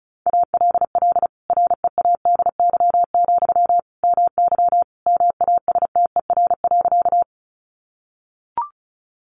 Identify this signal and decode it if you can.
Morse code, heard as a keyed tone, about 34 wpm, 700 Hz. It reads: ALL READY, MY MASTER.
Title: Sentences from Top 1300 Words 35wpm